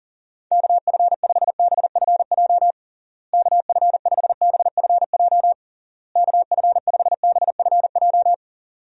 Morse code, heard as a keyed tone, about 40 wpm, 700 Hz. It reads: KF5BFJ KF5BFJ KF5BFJ
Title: USA Call Signs - 6 Characters - Repeated 3x - 40wpm